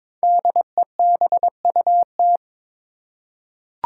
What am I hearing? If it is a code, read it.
Morse code, heard as a keyed tone, about 22 wpm, 700 Hz. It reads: DEBUT
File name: 5 Letter Words - First 3 Letters Given - 22wpm